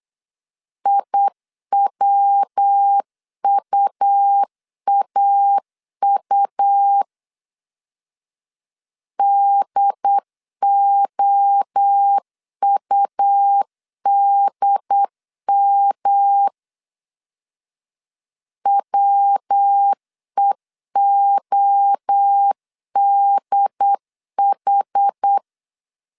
Morse code, heard as a keyed tone, about 8 wpm, 800 Hz. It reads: IWUAU DOUDM WEODH
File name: CW / DET / 1 3 1500C S B 8wpm 4141